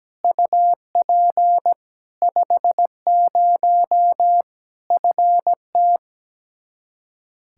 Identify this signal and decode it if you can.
Morse code, heard as a keyed tone, about 17 wpm, 700 Hz. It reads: UP 50 FT
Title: Practice QSO - Exchange 2 - Info - 17wpm